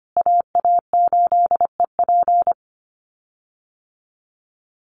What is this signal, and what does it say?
Morse code, heard as a keyed tone, about 25 wpm, 700 Hz. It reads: AA8EP
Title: USA Call Signs Repeated 3x 25wpm